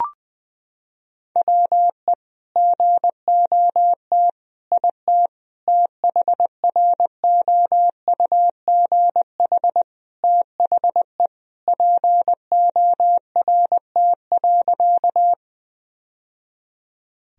Morse code, heard as a keyed tone, about 20 wpm, 700 Hz. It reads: WE GOT IT THROUGH THE PORT.
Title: Sentences from Top 1000 Words - Review Part 1 - 20wpm